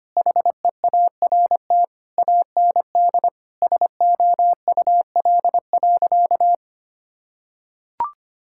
Morse code, heard as a keyed tone, about 25 wpm, 700 Hz. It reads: HEART AND SOUL.
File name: practice / 25wpm / Sentences with 9-Word Phrases - 25wpm